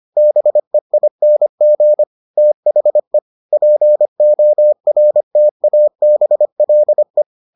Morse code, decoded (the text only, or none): BEING THE PORTABLE